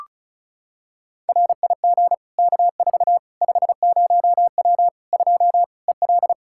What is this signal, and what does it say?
Morse code, heard as a keyed tone, about 36 wpm, 700 Hz. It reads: RIG K4 50W 2 EL
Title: Practice QSO - Exchange 2 - Info - 35wpm